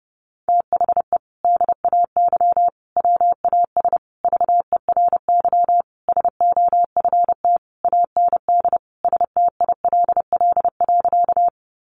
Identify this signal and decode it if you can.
Morse code, heard as a keyed tone, about 30 wpm, 700 Hz. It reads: THE DAY WAS VERY SOFT AND STILL.
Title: Sentences from Top 1500 Words 30wpm